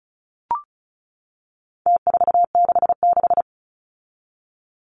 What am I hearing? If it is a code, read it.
Morse code, heard as a keyed tone, about 34 wpm, 700 Hz. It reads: T466
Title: Numbers - Contest Style - 4-Digit 35wpm